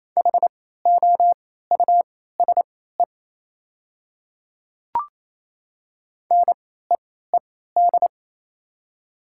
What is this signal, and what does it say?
Morse code, heard as a keyed tone, about 28 wpm, spaced as for 16 wpm, 700 Hz. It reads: HOUSE NEED
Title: 3x Character Spacing - Top 500 Words - 28wpm